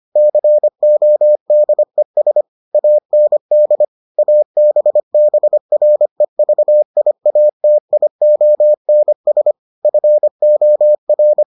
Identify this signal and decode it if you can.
Morse code, heard as a keyed tone, about 25 wpm, 600 Hz. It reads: CODES AND ABBREVIATIONS FOR